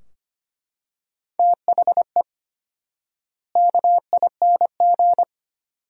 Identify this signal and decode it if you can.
Morse code, heard as a keyed tone, about 25 wpm, 700 Hz. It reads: THE KING
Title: 4x Word-Spacing - Sentences from Top 400 Words - 25wpm